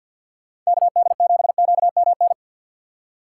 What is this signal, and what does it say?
Morse code, heard as a keyed tone, about 60 wpm, 700 Hz. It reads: KD7QMN
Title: Speed Racing - USA 2x3 Call Signs 50wpm